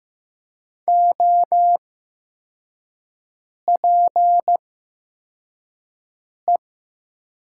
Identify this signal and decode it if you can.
Morse code, heard as a keyed tone, about 15 wpm, spaced as for 4 wpm, 700 Hz. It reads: OPE
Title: 8x Character Spacing - Top 300 Words - 15wpm